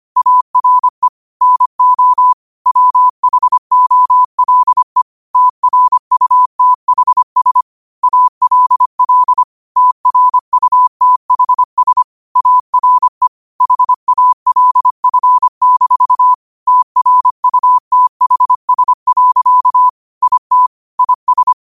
Morse code, heard as a keyed tone, about 25 wpm, 1000 Hz. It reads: ARE NO WHOLE TRUTHS ALL TRUTHS ARE HALF-TRUTHS. IT IS
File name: qotdfiles / 25wpm